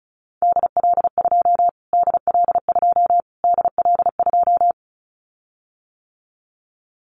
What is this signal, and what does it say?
Morse code, heard as a keyed tone, about 34 wpm, 700 Hz. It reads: DL2 DL2 DL2
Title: Callsign Prefixes for Top 25 DXCC - Repeated 3x - 35wpm